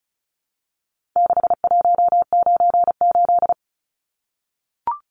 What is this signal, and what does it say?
Morse code, heard as a keyed tone, about 34 wpm, 700 Hz. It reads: 6198
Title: Numbers - 4-Digit 35wpm